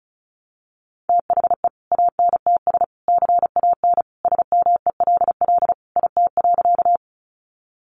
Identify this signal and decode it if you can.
Morse code, heard as a keyed tone, about 34 wpm, 700 Hz. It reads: THE ANTS CAN SMELL IT.
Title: Additional Easy Phrases 35wpm